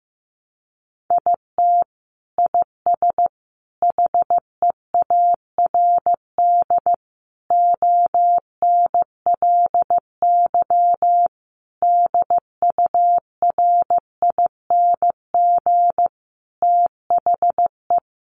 Morse code, heard as a keyed tone, about 15 wpm, 700 Hz. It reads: IT IS HEARD ONLY DURING THE
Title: Sentences from Top 1200 Words 15wpm